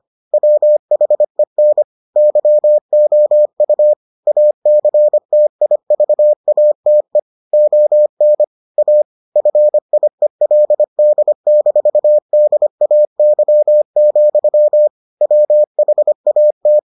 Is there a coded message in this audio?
Yes: Morse code from 0.3 to 16.8 s, heard as a keyed tone, about 25 wpm, 600 Hz. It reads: WHEN YOU ACTIVATE ON A FIELD-DAY, WHAT